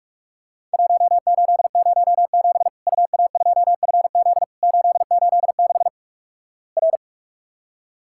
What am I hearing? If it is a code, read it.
Morse code, heard as a keyed tone, about 46 wpm, 700 Hz. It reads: 1807 UA2FZ 886 R
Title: Worked All Europe DX Contest - QTC Exchange - 45wpm